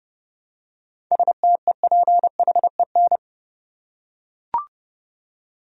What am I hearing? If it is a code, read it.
Morse code, heard as a keyed tone, about 30 wpm, 700 Hz. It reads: STEPHEN
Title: USA Common Names - 30wpm